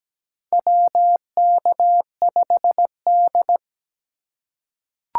Morse code, heard as a keyed tone, about 17 wpm, 700 Hz. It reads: WK5D
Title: USA Call Signs - 4x Character Spacing - 17wpm